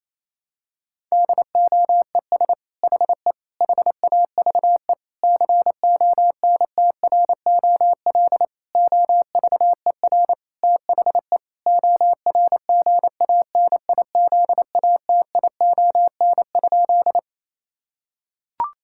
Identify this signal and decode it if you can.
Morse code, heard as a keyed tone, about 27 wpm, 700 Hz. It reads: DOES HE HAVE CONTROL OVER THE ORGANIZATION?
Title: Sentences from Top 1000 Words - Review Part 5 - 28wpm